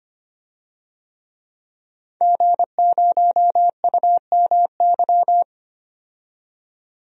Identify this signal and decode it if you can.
Morse code, heard as a keyed tone, about 25 wpm, 700 Hz. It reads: G0UMY
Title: DXCC-100 Call Signs - 5 Characters - 25wpm